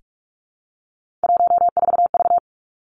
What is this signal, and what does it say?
Morse code, heard as a keyed tone, about 46 wpm, 700 Hz. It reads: 14V
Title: Three Challenging Letter-Number Combinations 45wpm